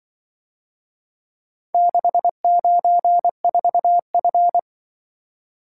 Morse code, heard as a keyed tone, about 25 wpm, 700 Hz. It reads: 694F